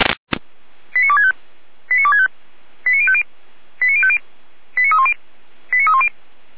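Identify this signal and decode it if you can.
Five-tone selective call (ZVEI), heard as quick runs of five tones, tones of about 70 ms. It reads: 89265 89265 89050 89050 89310 89310
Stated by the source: Five-tone ZVEI selective call with code 89246, received by SDR.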